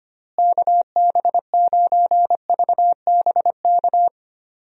Morse code, heard as a keyed tone, about 25 wpm, 700 Hz. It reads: KB9VBK
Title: USA Call Signs - 6 Characters - Repeated 2x - 25wpm